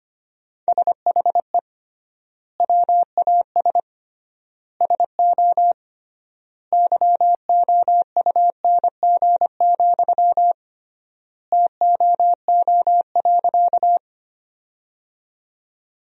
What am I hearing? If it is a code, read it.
Morse code, heard as a keyed tone, about 25 wpm, 700 Hz. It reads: SHE WAS SO YOUNG, TOO.
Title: 3x Word-Spacing - Sentences from Top 400 Words - 25wpm